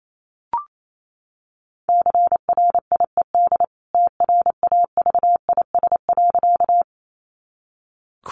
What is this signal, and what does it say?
Morse code, heard as a keyed tone, about 28 wpm, 700 Hz. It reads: CRIED TRAVIS.
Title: A Sound Of Thunder 28wpm